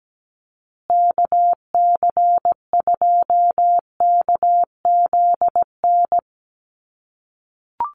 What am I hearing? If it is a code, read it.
Morse code, heard as a keyed tone, about 17 wpm, 700 Hz. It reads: KC2KZN
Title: USA Call Signs Repeated 3x 17wpm